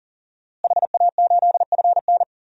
Morse code, heard as a keyed tone, about 40 wpm, 700 Hz. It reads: HA8FN
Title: DXCC-100 Call Signs - 5 Characters - Repeated 3x - 40wpm